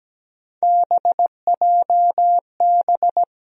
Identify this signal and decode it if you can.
Morse code, heard as a keyed tone, about 17 wpm, 700 Hz. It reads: BJB
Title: USA Call Sign Postfixes - 3 Letters 17wpm